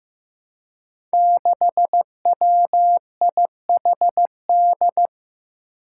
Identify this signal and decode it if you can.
Morse code, heard as a keyed tone, about 15 wpm, 700 Hz. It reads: 6WIHD
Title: Five Challenging Letter-Number Combinations 15wpm